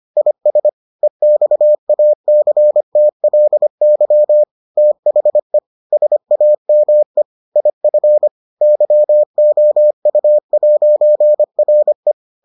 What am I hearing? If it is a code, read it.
Morse code, heard as a keyed tone, about 25 wpm, 600 Hz. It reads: IS EXACTLY THE SAME IF YOU'RE